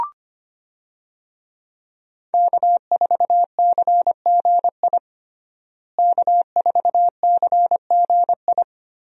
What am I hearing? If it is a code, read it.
Morse code, heard as a keyed tone, about 25 wpm, 700 Hz. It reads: K4CGI K4CGI
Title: USA Call Signs - 5 Characters - Repeated 3x - 25wpm